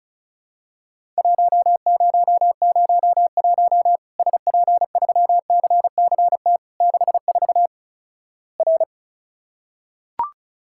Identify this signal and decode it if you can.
Morse code, heard as a keyed tone, about 36 wpm, 700 Hz. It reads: 1001 SP3CCT 64 R
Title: Worked All Europe DX Contest - QTC Exchange - 35wpm